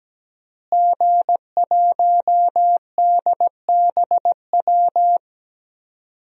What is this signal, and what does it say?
Morse code, heard as a keyed tone, about 17 wpm, 700 Hz. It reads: G1DBW
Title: DXCC-100 Call Signs - 5 Characters - Repeated 3x - 17wpm